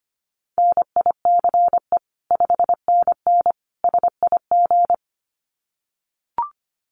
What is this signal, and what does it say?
Morse code, heard as a keyed tone, about 25 wpm, 700 Hz. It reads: NICE 5NN SIG
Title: Common QSO Phrases 25wpm